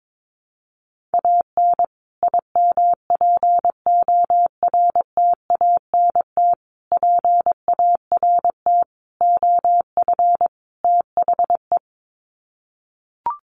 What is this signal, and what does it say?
Morse code, heard as a keyed tone, about 22 wpm, 700 Hz. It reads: AN IMPORTANT PART OF THE
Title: Sets of 5 Words Encore 22wpm